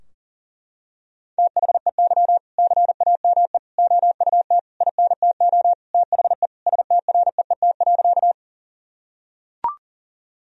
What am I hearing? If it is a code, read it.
Morse code, heard as a keyed tone, about 40 wpm, 700 Hz. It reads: THEY CAME OUT INTO THE STREET.
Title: Sentences from Top 1000 Words - Review Part 3 - 40wpm